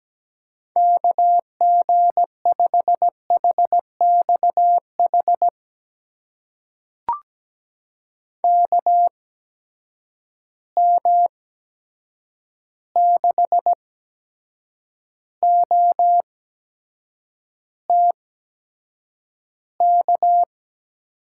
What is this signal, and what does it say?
Morse code, heard as a keyed tone, about 17 wpm, spaced as for 7 wpm, 700 Hz. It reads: KG5HXH KM6OTK
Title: USA Call Signs - 6 Characters - 8x Character Spacing - 17wpm